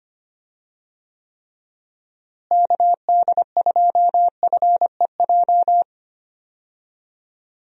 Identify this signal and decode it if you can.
Morse code, heard as a keyed tone, about 25 wpm, 700 Hz. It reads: KD2FEJ
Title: USA Call Signs - 6 Characters - Repeated 2x - 25wpm